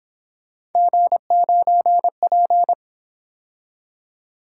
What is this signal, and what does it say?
Morse code, heard as a keyed tone, about 26 wpm, 700 Hz. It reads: G9P
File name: Speed-Racing - Three Letter-Number Combinations - 22wpm